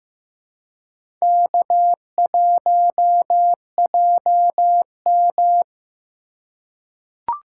Morse code, heard as a keyed tone, about 15 wpm, 700 Hz. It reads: K1JM
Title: USA Call Signs - 4 Characters - 3x Character Spacing - 15wpm